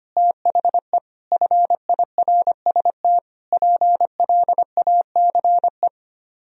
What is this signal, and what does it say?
Morse code, heard as a keyed tone, about 32 wpm, 700 Hz. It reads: THE FIRST PLACE